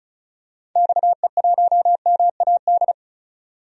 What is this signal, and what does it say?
Morse code, heard as a keyed tone, about 34 wpm, 700 Hz. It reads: XE1MAD
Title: DXCC-100 Call Signs - 6 Characters - 35wpm